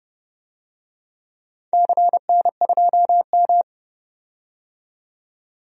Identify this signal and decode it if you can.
Morse code, heard as a keyed tone, about 30 wpm, 700 Hz. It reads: CN2M
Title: DXCC-100 Call Signs - 4 Characters - 30wpm